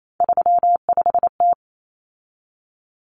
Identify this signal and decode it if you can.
Morse code, heard as a keyed tone, about 28 wpm, 700 Hz. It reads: 35T
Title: Common Cut Numbers - 3-Digit 28wpm